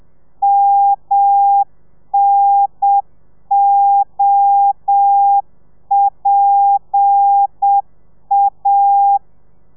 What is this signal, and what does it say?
Morse code, heard as a keyed tone, about 7 wpm, 800 Hz. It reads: MNOPA
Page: Lezione 1: audio a 30 CPM